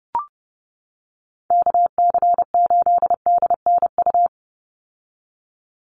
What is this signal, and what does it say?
Morse code, heard as a keyed tone, about 30 wpm, 700 Hz. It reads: KC8DNU
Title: USA Call Signs 30wpm